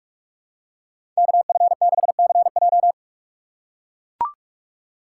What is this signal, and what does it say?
Morse code, heard as a keyed tone, about 43 wpm, 690 Hz. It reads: KF6CJ
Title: USA Call Signs - 5 Characters - 6x Character Spacing - 45wpm